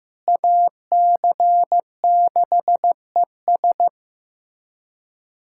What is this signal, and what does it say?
Morse code, heard as a keyed tone, about 16 wpm, 700 Hz. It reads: AC6ES
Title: USA Callsign DE Callsign - 5 Characters - 15wpm